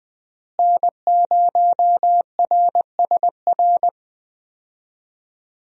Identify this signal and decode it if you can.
Morse code, heard as a keyed tone, about 20 wpm, 700 Hz. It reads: N0RSR
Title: USA Call Signs - 5 Characters - Repeated 2x - 20wpm